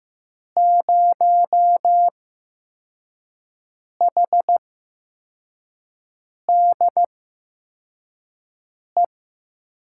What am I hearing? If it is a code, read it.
Morse code, heard as a keyed tone, about 15 wpm, spaced as for 4 wpm, 700 Hz. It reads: 0HDE
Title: USA Call Signs - 6 Characters - 8x Character Spacing - 15wpm